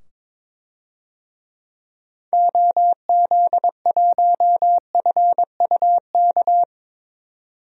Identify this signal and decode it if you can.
Morse code, heard as a keyed tone, about 21 wpm, 700 Hz. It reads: OZ1FUK
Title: DXCC-100 Call Signs - 6 Characters - 22wpm